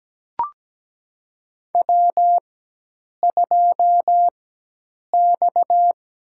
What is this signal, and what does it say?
Morse code, heard as a keyed tone, about 17 wpm, spaced as for 8 wpm, 700 Hz. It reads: W2X
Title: USA Call Signs - 4x Character Spacing - 17wpm